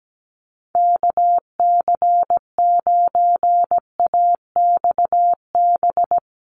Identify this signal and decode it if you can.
Morse code, heard as a keyed tone, about 17 wpm, 700 Hz. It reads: KC9AXB
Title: Common QSO Phrases Encore 17wpm